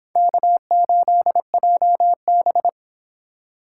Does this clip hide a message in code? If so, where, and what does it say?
Yes: Morse code from 0.2 to 2.7 s, heard as a keyed tone, about 26 wpm, 700 Hz. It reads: K8JB